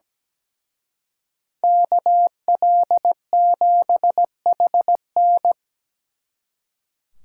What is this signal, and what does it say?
Morse code, heard as a keyed tone, about 20 wpm, 700 Hz. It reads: KL7HN
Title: Speed Racing - USA 2x2 Call Signs 17wpm